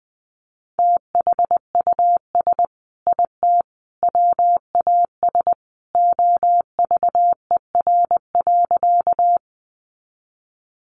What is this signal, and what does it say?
Morse code, heard as a keyed tone, about 20 wpm, 700 Hz. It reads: THUS IT WAS OVER.
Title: Sentences from Top 800 Words 20wpm